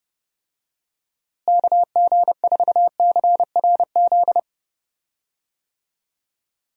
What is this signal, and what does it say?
Morse code, heard as a keyed tone, about 30 wpm, 700 Hz. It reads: KG4CRZ
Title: USA Call Signs - 6 Characters - Repeated 2x - 30wpm